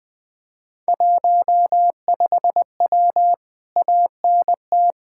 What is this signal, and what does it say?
Morse code, heard as a keyed tone, about 20 wpm, 700 Hz. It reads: 15W ANT
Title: Speed-Racing Common QSO Phrases - 20wpm